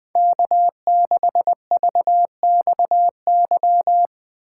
Morse code, heard as a keyed tone, about 20 wpm, 700 Hz. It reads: K6VXY
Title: USA Call Signs - 5 Characters - 4x Character Spacing - 20wpm